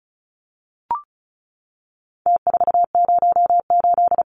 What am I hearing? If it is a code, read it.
Morse code, heard as a keyed tone, about 34 wpm, 700 Hz. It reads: T408
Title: Numbers - Contest Style - 4-Digit 35wpm